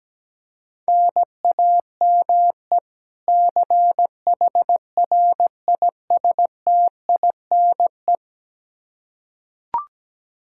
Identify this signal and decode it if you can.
Morse code, heard as a keyed tone, about 17 wpm, 700 Hz. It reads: NAME CHRISTINE
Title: Practice QSO - Exchange 1 - Beginning - 17wpm